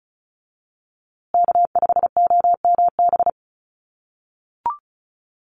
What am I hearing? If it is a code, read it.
Morse code, heard as a keyed tone, about 34 wpm, 700 Hz. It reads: K5OMB
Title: USA Call Signs Repeated 2x 35wpm